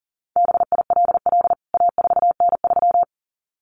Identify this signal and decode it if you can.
Morse code, heard as a keyed tone, about 37 wpm, 700 Hz. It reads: BILL A4N3
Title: CWT Exchange 40wpm